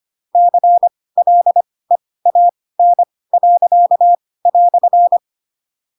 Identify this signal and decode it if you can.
Morse code, heard as a keyed tone, about 25 wpm, spaced as for 18 wpm, 700 Hz. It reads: CLEAN."